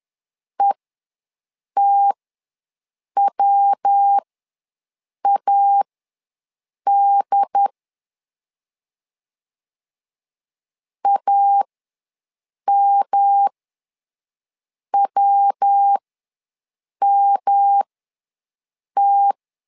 Morse code, heard as a keyed tone, about 10 wpm, spaced as for 6 wpm, 800 Hz. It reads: ETWAD AMWMT
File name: LEZIONI / 3 MWRO / 3 MW ws 10wpm 2213